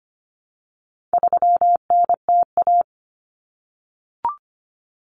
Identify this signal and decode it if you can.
Morse code, heard as a keyed tone, about 25 wpm, 700 Hz. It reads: 3NTA
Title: Common Cut Numbers - 4-Digit 25wpm